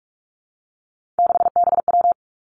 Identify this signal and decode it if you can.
Morse code, heard as a keyed tone, about 46 wpm, 700 Hz. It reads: 6BW
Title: Three Challenging Letter-Number Combinations 45wpm